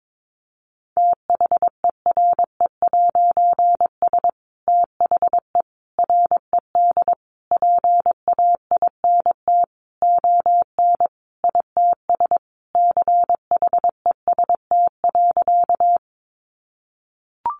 Morse code, heard as a keyed tone, about 22 wpm, 700 Hz. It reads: THERE'S THE RED PAINT ON ITS CHEST.
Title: A Sound Of Thunder 22wpm